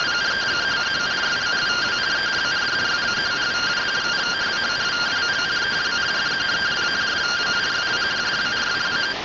Two-tone teleprinter signal, about 170 Hz shift.